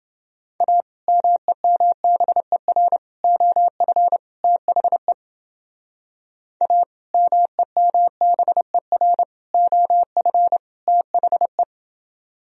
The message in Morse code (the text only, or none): A MEMBER OF THE A MEMBER OF THE